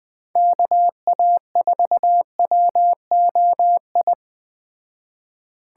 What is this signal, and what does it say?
Morse code, heard as a keyed tone, about 20 wpm, 700 Hz. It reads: KA4WOI
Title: Speed Racing - USA Call Signs 17wpm